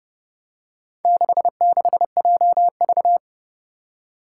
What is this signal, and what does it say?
Morse code, heard as a keyed tone, about 30 wpm, 700 Hz. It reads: 66JV
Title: Four Challenging Letter-Number Combinations 30wpm